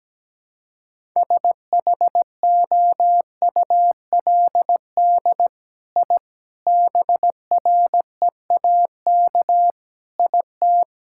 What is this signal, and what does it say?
Morse code, heard as a keyed tone, about 17 wpm, 700 Hz. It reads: SHOULD I BREAK IT
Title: Sentences from Top 1000 Words - Review Part 6 - 17wpm